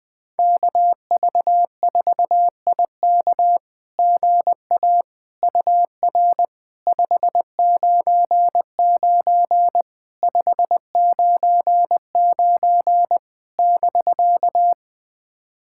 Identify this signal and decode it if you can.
Morse code, heard as a keyed tone, about 21 wpm, 700 Hz. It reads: KV4IK GA UR 599 599 <BK>
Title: SOTA QSO Practice 20wpm